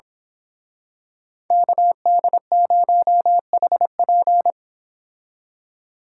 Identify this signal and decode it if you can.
Morse code, heard as a keyed tone, about 25 wpm, 700 Hz. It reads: KD0HP
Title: Speed Racing - USA Call Signs 22wpm